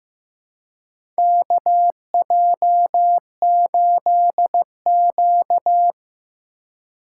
Morse code, heard as a keyed tone, about 15 wpm, 700 Hz. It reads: KJ8Q
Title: USA Call Signs - 4 Characters - 2x Character Spacing - 15wpm